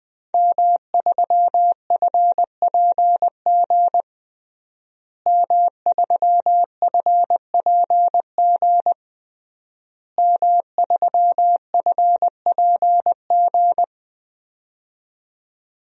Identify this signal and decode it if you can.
Morse code, heard as a keyed tone, about 21 wpm, 700 Hz. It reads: M3FPG M3FPG M3FPG
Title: DXCC-100 Call Signs - 5 Characters - Repeated 3x - 20wpm